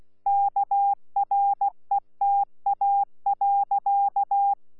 Morse code, heard as a keyed tone, about 16 wpm, 800 Hz. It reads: KRETA.
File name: Z PI4VRZ / PI4VRZ 14-jun-2025